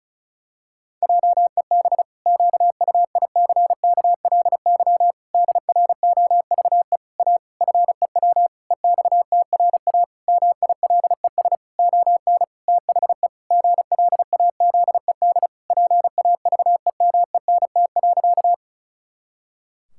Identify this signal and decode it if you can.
Morse code, heard as a keyed tone, about 34 wpm, 700 Hz. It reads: JEB QUICKLY DROVE A FEW EXTRA MILES ON THE GLAZED PAVEMENT.
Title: Pangram Challenge v2 - 35wpm